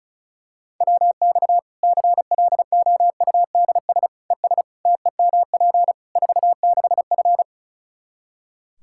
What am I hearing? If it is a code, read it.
Morse code, heard as a keyed tone, about 36 wpm, 700 Hz. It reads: WX CLOUDS ES TEMP 46F
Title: Practice QSO - Exchange 2 - Info - 35wpm